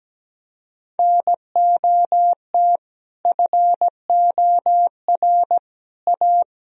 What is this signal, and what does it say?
Morse code, heard as a keyed tone, about 17 wpm, 700 Hz. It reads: NOT FOR A
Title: Sentences from Top 1000 Words - Review Part 1 - 17wpm